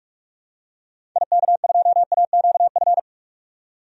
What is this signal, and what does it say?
Morse code, heard as a keyed tone, about 46 wpm, 700 Hz. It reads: IK2AQF
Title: DXCC-100 Call Signs - 6 Characters - 45wpm